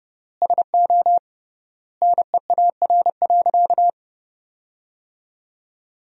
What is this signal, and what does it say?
Morse code, heard as a keyed tone, about 30 wpm, 700 Hz. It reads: SO NEAR.
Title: 3x Word-Spacing - Sentences from Top 300 Words - 30wpm